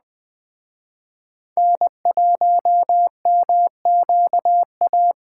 Morse code, heard as a keyed tone, about 21 wpm, 700 Hz. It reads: N1MQA
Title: Speed Racing - USA 1x3 Call Signs 17wpm